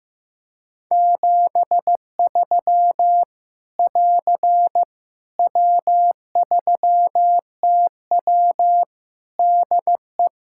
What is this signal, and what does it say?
Morse code, heard as a keyed tone, about 15 wpm, 700 Hz. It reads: 73 <AR> W3TW DE
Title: Practice QSO - Exchange 4 - Ending - 15wpm